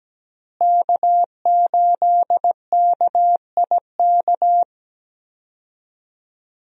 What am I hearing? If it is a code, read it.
Morse code, heard as a keyed tone, about 17 wpm, 700 Hz. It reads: K8KIK
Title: USA Call Signs - 5 Characters - Repeated 2x - 17wpm